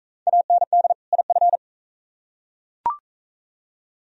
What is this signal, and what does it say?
Morse code, heard as a keyed tone, about 41 wpm, 700 Hz. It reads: AND IF